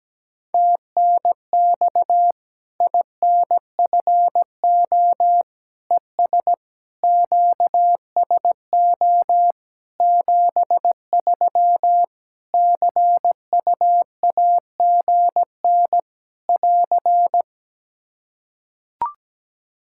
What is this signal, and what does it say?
Morse code, heard as a keyed tone, about 17 wpm, 700 Hz. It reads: TNX INFO ES QSO 73 CU AGN <AR>
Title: Practice QSO - Exchange 4 - Ending - 17wpm